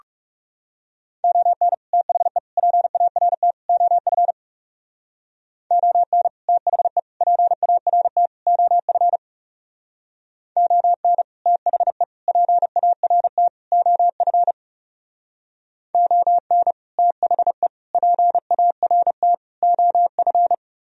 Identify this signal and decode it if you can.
Morse code, heard as a keyed tone, about 40 wpm, 700 Hz. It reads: ON THE PART OF ON THE PART OF ON THE PART OF ON THE PART OF